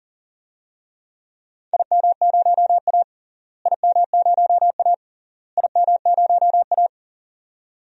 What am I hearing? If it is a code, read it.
Morse code, heard as a keyed tone, about 40 wpm, 700 Hz. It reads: IM0A IM0A IM0A
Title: DXCC-100 Call Signs - 4 Characters - Repeated 3x - 40wpm